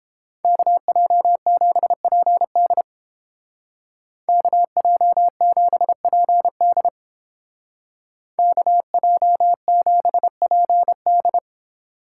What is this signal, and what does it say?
Morse code, heard as a keyed tone, about 29 wpm, 700 Hz. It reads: KJ7PD KJ7PD KJ7PD